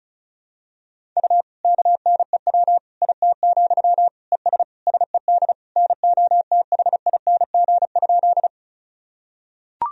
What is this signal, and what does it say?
Morse code, heard as a keyed tone, about 36 wpm, 700 Hz. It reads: U KNEW IT, ES SED NOTHING?
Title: Sentences from Top 500 Words and Abbreviations 35wpm